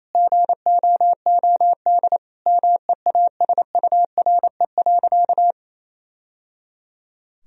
Morse code, heard as a keyed tone, about 28 wpm, 700 Hz. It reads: GOOD MEASURE.
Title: Sentences from Top 1000 Words - Review Part 2 - 28wpm